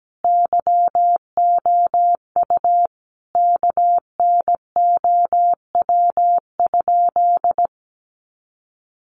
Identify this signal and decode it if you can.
Morse code, heard as a keyed tone, about 17 wpm, 700 Hz. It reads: YOU KNOW?
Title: Sentences from Top 100 Words 17wpm